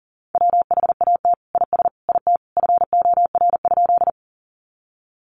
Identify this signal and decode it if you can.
Morse code, heard as a keyed tone, about 40 wpm, 700 Hz. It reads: WHAT IS IT FOR?
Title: Sentences from Top 100 Words 40wpm